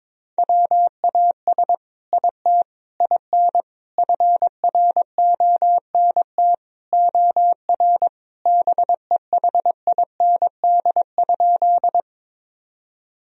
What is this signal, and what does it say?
Morse code, heard as a keyed tone, about 22 wpm, 700 Hz. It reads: WAS IT IN FRONT OR BEHIND?
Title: Sentences from Top 1000 Words - Review Part 2 - 22wpm